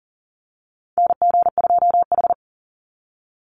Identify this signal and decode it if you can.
Morse code, heard as a keyed tone, about 40 wpm, 700 Hz. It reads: NG2H
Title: Call Signs - 40wpm